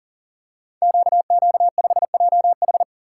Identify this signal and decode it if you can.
Morse code, heard as a keyed tone, about 40 wpm, 690 Hz. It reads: QQ5JH
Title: Code Groups - 5 Letter:Numbers - 40wpm